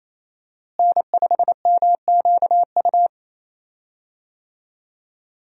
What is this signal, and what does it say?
Morse code, heard as a keyed tone, about 28 wpm, 700 Hz. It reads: N5MQU
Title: USA Call Signs - 5 Characters - Repeated 2x - 28wpm